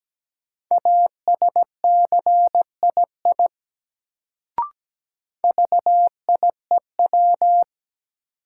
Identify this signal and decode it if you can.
Morse code, heard as a keyed tone, about 17 wpm, 700 Hz. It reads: ASCII VIEW
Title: Sets of 5 Words - New Word Drill - 17wpm